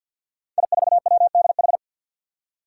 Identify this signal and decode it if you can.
Morse code, heard as a keyed tone, about 49 wpm, 700 Hz. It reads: I4WDH
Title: Five Challenging Letter-Number Combinations 50wpm